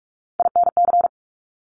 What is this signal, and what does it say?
Morse code, heard as a keyed tone, about 46 wpm, 700 Hz. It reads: INC
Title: Top Three Letter Combinations in Words 45wpm